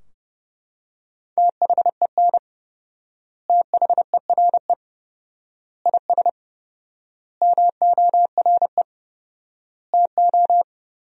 Morse code, heard as a keyed tone, about 30 wpm, 700 Hz. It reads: THEN THERE IS MORE TO
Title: 4x Word-Spacing - Sentences from Top 100 Words - 30wpm